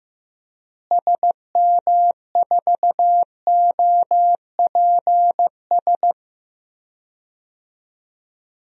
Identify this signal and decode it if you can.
Morse code, heard as a keyed tone, about 15 wpm, 700 Hz. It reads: SM4OPS
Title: DXCC-100 Call Signs - 6 Characters - Repeated 3x - 15wpm